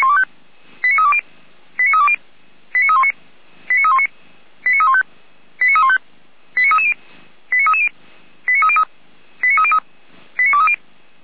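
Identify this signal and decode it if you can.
Five-tone selective call (ZVEI), heard as quick runs of five tones, tones of about 70 ms. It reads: 89320 89320 89319 89319 89315 89315 89330 89330 89303 89303 89230
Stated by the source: ZVEI five-tone call 89110 received by SDR.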